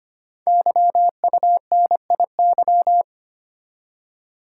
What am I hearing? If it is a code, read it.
Morse code, heard as a keyed tone, about 25 wpm, 700 Hz. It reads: YUNIY
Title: Code Groups - 5 Letters - 25wpm